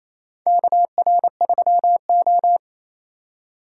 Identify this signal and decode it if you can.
Morse code, heard as a keyed tone, about 28 wpm, 700 Hz. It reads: KR3O
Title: USA Call Signs - 4 Characters - 4x Character Spacing - 28wpm